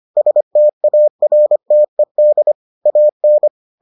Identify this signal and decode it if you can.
Morse code, heard as a keyed tone, about 25 wpm, 600 Hz. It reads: STARTED AN